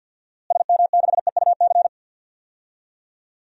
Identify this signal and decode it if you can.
Morse code, heard as a keyed tone, about 47 wpm, 700 Hz. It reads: SM6EUC
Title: DXCC-100 Call Signs - 6 Characters - Repeated 2x - 50wpm